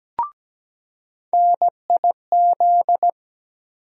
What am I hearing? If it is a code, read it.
Morse code, heard as a keyed tone, about 17 wpm, 700 Hz. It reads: NIZ
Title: USA Call Sign Postfixes - 3 Letters 17wpm